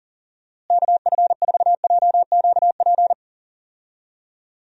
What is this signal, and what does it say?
Morse code, heard as a keyed tone, about 40 wpm, 700 Hz. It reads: KF4JQP
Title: USA Call Signs - 6 Characters - Repeated 2x - 40wpm